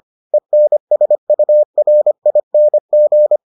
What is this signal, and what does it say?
Morse code, heard as a keyed tone, about 25 wpm, 600 Hz. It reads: ENSURING